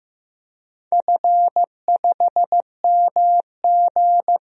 Morse code, heard as a keyed tone, about 15 wpm, 700 Hz. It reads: F5MG
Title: DXCC-100 Call Signs - 4 Characters - Repeated 2x - 15wpm